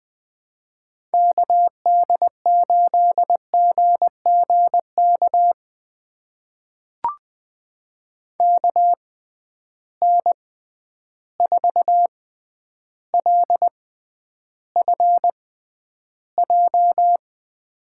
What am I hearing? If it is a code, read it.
Morse code, heard as a keyed tone, about 20 wpm, spaced as for 10 wpm, 700 Hz. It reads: KD8GGK KN4LFJ
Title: USA Call Signs - 6 Characters - 6x Character Spacing - 20wpm